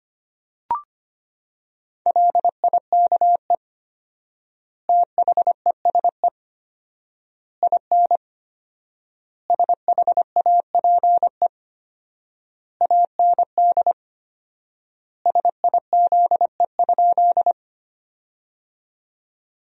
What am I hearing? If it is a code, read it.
Morse code, heard as a keyed tone, about 25 wpm, 700 Hz. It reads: LIKE THESE IN SHAPE AND SIZE?
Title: 4x Word-Spacing - Sentences from Top 500 Words - 25wpm